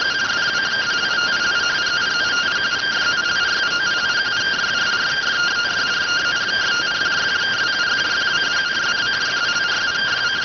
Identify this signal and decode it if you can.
Two-tone teleprinter signal, about 170 Hz shift.